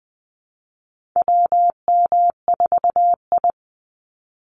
Morse code, heard as a keyed tone, about 20 wpm, 700 Hz. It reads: WM4I
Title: Call Signs Repeated 3x - 20wpm